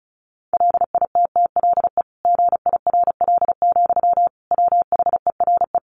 Morse code, heard as a keyed tone, about 34 wpm, 700 Hz. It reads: LITTLE GIRL, WHERE
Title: Sentences from Top 400 Words 35wpm